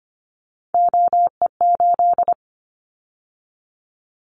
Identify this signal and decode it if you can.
Morse code, heard as a keyed tone, about 25 wpm, 700 Hz. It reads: OE8
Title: Callsign Prefixes for Top 50 DXCC - 25wpm